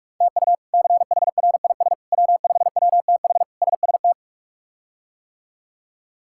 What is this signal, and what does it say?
Morse code, heard as a keyed tone, about 43 wpm, 700 Hz. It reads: TU CHRIS W5WTH SST
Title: K1USN SST QSO Practice - 45wpm